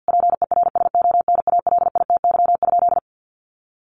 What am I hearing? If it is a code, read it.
Morse code, heard as a keyed tone, about 50 wpm, 700 Hz. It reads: PERSONALITY?
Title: Sentences from Top 1400 Words 50wpm